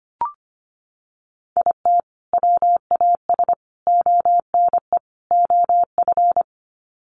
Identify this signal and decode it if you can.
Morse code, heard as a keyed tone, about 25 wpm, 700 Hz. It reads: IT WAS ONE OF
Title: Sets of 4 Words 25wpm